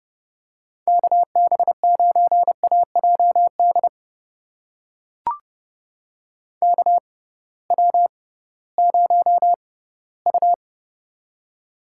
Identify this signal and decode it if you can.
Morse code, heard as a keyed tone, about 30 wpm, spaced as for 17 wpm, 700 Hz. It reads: KB9AJD KW0U
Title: USA Call Signs - 6x Character Spacing - 30wpm